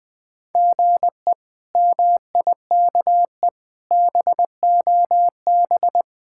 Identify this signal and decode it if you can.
Morse code, heard as a keyed tone, about 20 wpm, 700 Hz. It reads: GE MIKE BOB
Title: K1USN SST QSO Practice - 20wpm